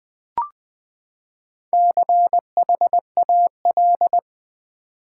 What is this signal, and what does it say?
Morse code, heard as a keyed tone, about 19 wpm, 700 Hz. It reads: CHAL
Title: Build A Word - 4 Letters At A Time - 11-Letter Words - 20wpm